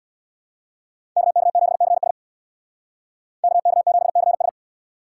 Morse code, heard as a keyed tone, about 63 wpm, 700 Hz. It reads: KK7CS KK7CS